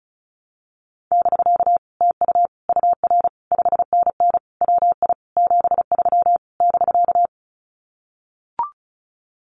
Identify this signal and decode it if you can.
Morse code, heard as a keyed tone, about 36 wpm, 700 Hz. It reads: <BK> TU UR 5NN WI 73 <BK>
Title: Common QSO Phrases Encore 35wpm